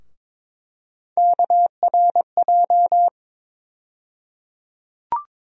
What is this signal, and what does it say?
Morse code, heard as a keyed tone, about 22 wpm, 700 Hz. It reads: KRJ
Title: USA Call Sign Postfixes - 3 Letters 22wpm